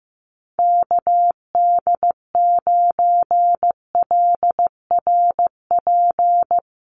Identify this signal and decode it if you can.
Morse code, heard as a keyed tone, about 15 wpm, 700 Hz. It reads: KD9LRP
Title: USA Call Signs 15wpm